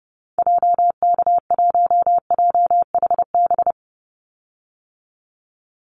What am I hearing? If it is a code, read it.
Morse code, heard as a keyed tone, about 30 wpm, 700 Hz. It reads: JK1JHB
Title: Call Signs Repeated 2x - 30wpm